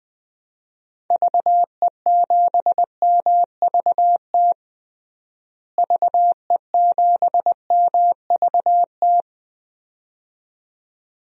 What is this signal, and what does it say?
Morse code, heard as a keyed tone, about 20 wpm, 700 Hz. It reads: VE7MVT VE7MVT
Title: DXCC-100 Call Signs - 6 Characters - Repeated 2x - 20wpm